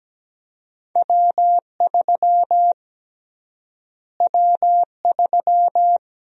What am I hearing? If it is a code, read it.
Morse code, heard as a keyed tone, about 17 wpm, 700 Hz. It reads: W3 W3
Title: USA Call Sign Prefixes - Repeated 2x - 17wpm